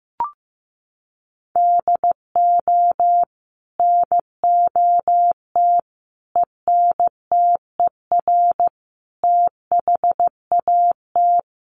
Morse code, heard as a keyed tone, about 15 wpm, 700 Hz. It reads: DO NOT ENTER THAT
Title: Sentences from Top 1300 Words 15wpm